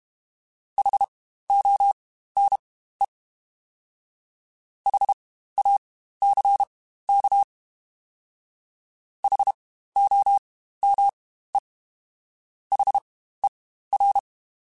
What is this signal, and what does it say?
Morse code, heard as a keyed tone, about 32 wpm, spaced as for 15 wpm, 800 Hz. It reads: HONE HACK HOME HER
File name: Spaced Alphas / Spaced Alphas mp3s / Hs sp @30